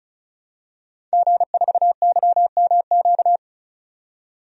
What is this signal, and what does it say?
Morse code, heard as a keyed tone, about 36 wpm, 700 Hz. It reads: G4YMQ
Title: DXCC-100 Call Signs - 5 Characters - 35wpm